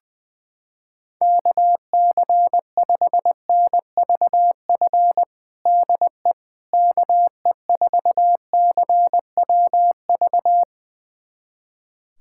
Morse code, heard as a keyed tone, about 21 wpm, 700 Hz. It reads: KC5NVF DE KE4CWV
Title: USA Callsign DE Callsign 20wpm